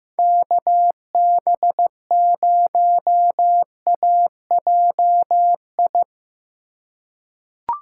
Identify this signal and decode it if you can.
Morse code, heard as a keyed tone, about 16 wpm, 700 Hz. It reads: KB0AJI
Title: USA Call Signs - 6 Characters - 4x Character Spacing - 15wpm